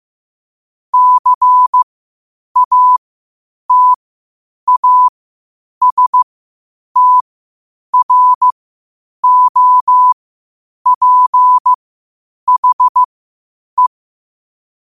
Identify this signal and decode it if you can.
Morse code, heard as a keyed tone, about 15 wpm, spaced as for 8 wpm, 1000 Hz. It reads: CATASTROPHE